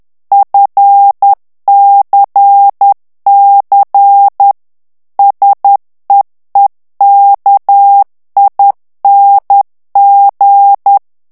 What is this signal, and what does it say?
Morse code, heard as a keyed tone, about 10 wpm, 800 Hz. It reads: FCC SEEKING
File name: ANGLAIS texte clair 3 vitesse 10 mots minute